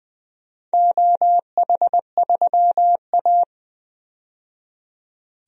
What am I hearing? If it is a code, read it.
Morse code, heard as a keyed tone, about 20 wpm, 700 Hz. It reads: OH3A
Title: DXCC-100 Call Signs - 4 Characters - 20wpm